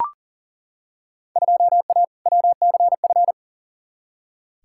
Morse code, heard as a keyed tone, about 40 wpm, 700 Hz. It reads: 2A WCF
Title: Field Day Exchanges Encore - 40wpm